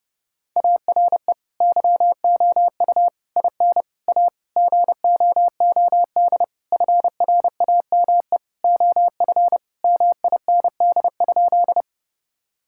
Morse code, heard as a keyed tone, about 32 wpm, 700 Hz. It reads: ARE YOU IN A GOOD FRAME OF MIND?
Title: Sentences from Top 1000 Words - Review Part 6 - 30wpm